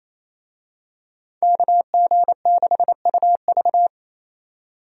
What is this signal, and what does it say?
Morse code, heard as a keyed tone, about 28 wpm, 700 Hz. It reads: KG6UV
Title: USA Call Signs - 5 Characters - 28wpm